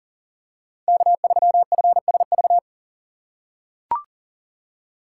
Morse code, heard as a keyed tone, about 39 wpm, 700 Hz. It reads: K3FSV
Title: USA Call Signs - 6x Character Spacing - 40wpm